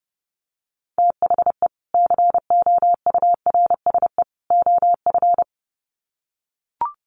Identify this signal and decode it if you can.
Morse code, heard as a keyed tone, about 28 wpm, 700 Hz. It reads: THE COURSE OF
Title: Sets of 3 Words 30wpm